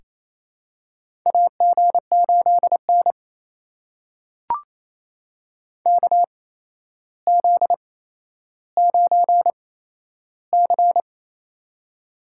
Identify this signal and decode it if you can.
Morse code, heard as a keyed tone, about 28 wpm, spaced as for 12 wpm, 700 Hz. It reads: AG8N KZ9C
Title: USA Call Signs - 4 Characters - 8x Character Spacing - 28wpm